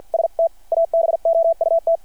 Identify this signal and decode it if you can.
Morse code, heard as a keyed tone, about 51 wpm, spaced as for 45 wpm, 660 Hz. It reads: ST ABOUT
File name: archived / CRSnet